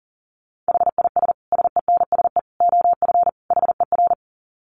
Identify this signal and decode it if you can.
Morse code, heard as a keyed tone, about 40 wpm, 700 Hz. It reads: HIS SENSE OF HER
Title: Dit Challenge 40wpm